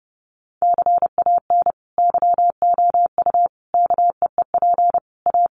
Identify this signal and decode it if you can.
Morse code, heard as a keyed tone, about 30 wpm, 700 Hz. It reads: CAN YOU KEEP A
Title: Sentences from Top 1200 Words 30wpm